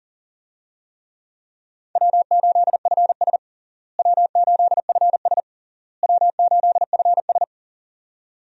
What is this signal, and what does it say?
Morse code, heard as a keyed tone, about 40 wpm, 700 Hz. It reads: W8FS W8FS W8FS
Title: USA Call Signs - 4 Characters - Repeated 3x - 40wpm